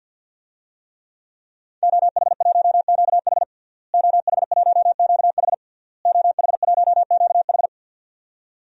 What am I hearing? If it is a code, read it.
Morse code, heard as a keyed tone, about 48 wpm, 700 Hz. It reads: OH1QH OH1QH OH1QH
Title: DXCC-100 Call Signs - 5 Characters - Repeated 3x - 50wpm